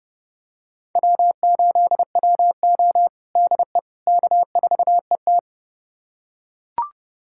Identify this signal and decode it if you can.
Morse code, heard as a keyed tone, about 32 wpm, 700 Hz. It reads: W8WO DE K4ET
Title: USA Callsign DE Callsign - 4 Characters - 30wpm